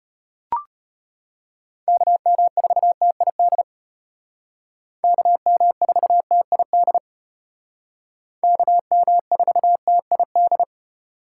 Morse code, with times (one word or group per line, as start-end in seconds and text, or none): 1.9-3.6 KM4TID
5.0-7.0 KM4TID
8.4-10.6 KM4TID